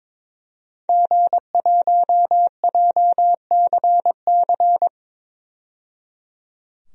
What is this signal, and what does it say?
Morse code, heard as a keyed tone, about 22 wpm, 700 Hz. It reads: G1JCC
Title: DXCC-100 Call Signs - 5 Characters - 22wpm